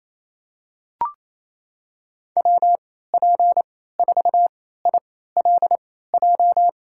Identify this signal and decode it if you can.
Morse code, heard as a keyed tone, about 28 wpm, spaced as for 16 wpm, 700 Hz. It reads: WP4ILJ
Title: USA Call Signs - 6 Characters - 3x Character Spacing - 28wpm